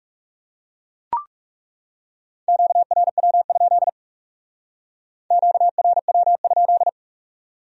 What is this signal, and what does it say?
Morse code, heard as a keyed tone, about 39 wpm, 700 Hz. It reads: QRW? QRW?